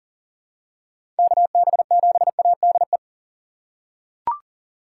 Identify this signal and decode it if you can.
Morse code, heard as a keyed tone, about 40 wpm, 700 Hz. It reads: KB7ADE
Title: USA Call Signs - 6 Characters - 4x Character Spacing - 40wpm